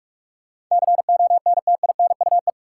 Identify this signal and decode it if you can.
Morse code, heard as a keyed tone, about 46 wpm, 700 Hz. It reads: CONTINUE
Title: Sentences from Top 1100 Words - New Word Drill Alt - 45wpm